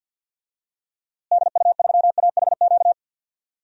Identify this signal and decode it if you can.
Morse code, heard as a keyed tone, about 51 wpm, 700 Hz. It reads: DU3AHQ
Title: DXCC-100 Call Signs - 6 Characters - Repeated 2x - 50wpm